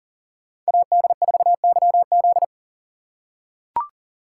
Morse code, heard as a keyed tone, about 40 wpm, 700 Hz. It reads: AD4YZ
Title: USA Call Signs - 5 Characters - 4x Character Spacing - 40wpm